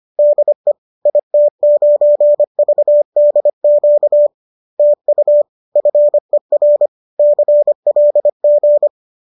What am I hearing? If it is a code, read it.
Morse code, heard as a keyed tone, about 25 wpm, 590 Hz. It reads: DE IT9VDQ TU FER CLG